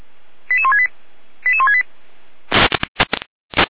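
Five-tone selective call (ZVEI), heard as quick runs of five tones, tones of about 70 ms. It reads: 88168 88168
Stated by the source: ZVEI five-tone call received on SDR.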